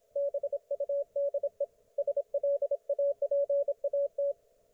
Morse code, heard as a keyed tone, about 27 wpm, 560 Hz, with radio noise. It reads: BUDE SLAPAT